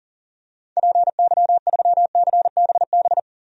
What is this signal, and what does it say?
Morse code, heard as a keyed tone, about 40 wpm, 700 Hz. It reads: PY3CBB
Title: DXCC-100 Call Signs - 6 Characters - 40wpm